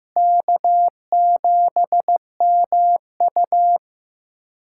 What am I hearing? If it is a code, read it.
Morse code, heard as a keyed tone, about 15 wpm, 700 Hz. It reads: K7MU
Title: USA Callsign DE Callsign - 4 Characters - 15wpm